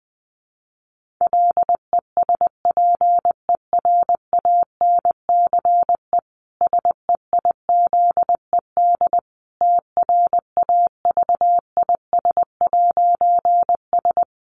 Morse code, heard as a keyed tone, about 20 wpm, 700 Hz. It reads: LESPERANCE SEIZED TRAVIS'S
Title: A Sound Of Thunder 20wpm